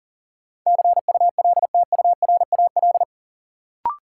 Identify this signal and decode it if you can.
Morse code, heard as a keyed tone, about 40 wpm, 700 Hz. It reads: CULTURAL
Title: Sentences from Top 1500 Words - New Word Drill Alt - 40wpm